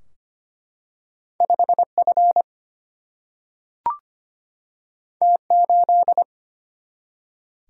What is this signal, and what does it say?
Morse code, heard as a keyed tone, about 25 wpm, 700 Hz. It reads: 5F T8
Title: Single-Letter Single-Number 25wpm